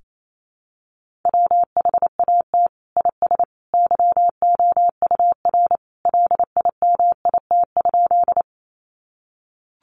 Morse code, heard as a keyed tone, about 28 wpm, 700 Hz. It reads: WHAT IS YOUR LIMIT?
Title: Sentences from Top 1500 Words 28wpm